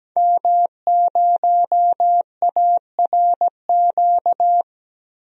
Morse code, heard as a keyed tone, about 17 wpm, 700 Hz. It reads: M0ARQ
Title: DXCC-100 Call Signs - 5 Characters - Repeated 2x - 17wpm